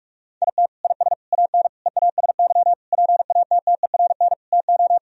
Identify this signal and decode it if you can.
Morse code, heard as a keyed tone, about 46 wpm, 700 Hz. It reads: IT IS AN EASY PATTERN TO
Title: Sentences from Top 1000 Words - Review Part 2 - 45wpm